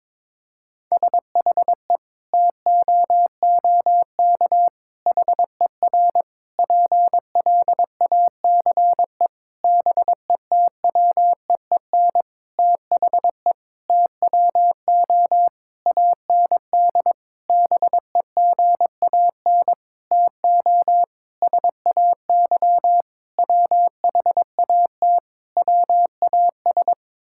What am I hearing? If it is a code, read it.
Morse code, heard as a keyed tone, about 22 wpm, 700 Hz. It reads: SHE TOOK HER PLACE BETWEEN THE TWO AND BEGAN TO SAY WHAT WAS
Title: Sentences from Top 1000 Words - Review Part 5 - 22wpm